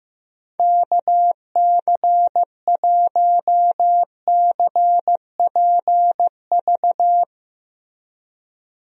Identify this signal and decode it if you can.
Morse code, heard as a keyed tone, about 15 wpm, 700 Hz. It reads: KC1CPV
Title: USA Call Signs - 6 Characters - Repeated 3x - 15wpm